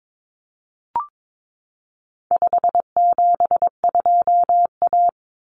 Morse code, heard as a keyed tone, about 22 wpm, 700 Hz. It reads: 572A
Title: Common Cut Numbers - 4-Digit 22wpm